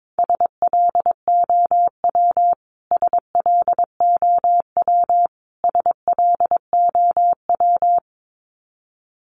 Morse code, heard as a keyed tone, about 22 wpm, 700 Hz. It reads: SLOW SLOW SLOW
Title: Top 500 Words Repeated 22wpm